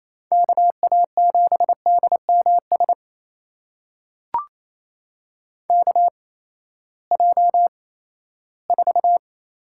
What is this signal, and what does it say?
Morse code, heard as a keyed tone, about 28 wpm, spaced as for 16 wpm, 700 Hz. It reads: KA7DMS KJ4
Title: USA Call Signs - 6 Characters - 8x Character Spacing - 28wpm